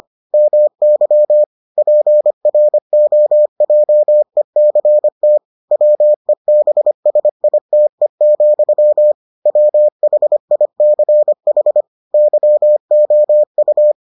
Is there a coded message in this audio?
Yes: Morse code from 0.3 to 13.9 s, heard as a keyed tone, about 25 wpm, 600 Hz. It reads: MY PROJECT WEBSITE, WHICH YOU